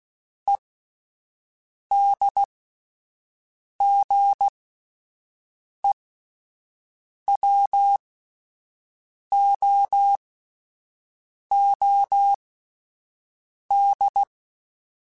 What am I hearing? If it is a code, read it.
Morse code, heard as a keyed tone, about 16 wpm, spaced as for 5 wpm, 800 Hz. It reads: EDGEWOOD